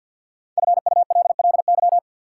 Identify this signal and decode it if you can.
Morse code, heard as a keyed tone, about 52 wpm, 700 Hz. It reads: FULLY
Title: Sentences from Top 1300 Words - New Word Drill Alt - 50wpm